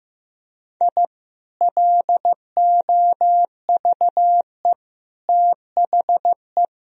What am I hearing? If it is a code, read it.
Morse code, heard as a keyed tone, about 15 wpm, 700 Hz. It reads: I LOVE THE
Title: Sentences from Top 1000 Words - Review Part 3 - 15wpm